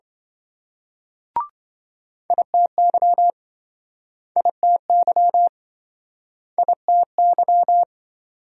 Morse code, heard as a keyed tone, about 27 wpm, 700 Hz. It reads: ITY ITY ITY